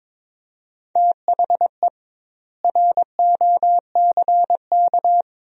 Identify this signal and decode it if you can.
Morse code, heard as a keyed tone, about 22 wpm, 700 Hz. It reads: THE ROCK
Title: 2x Word-Spacing - Sentences from Top 400 Words - 22wpm